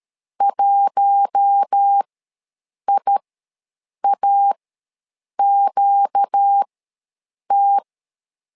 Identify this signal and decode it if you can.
Morse code, heard as a keyed tone, about 13 wpm, spaced as for 7 wpm, 800 Hz. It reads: 1IAQT
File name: LEZIONI / 4 Q1 ws 12wpm 2129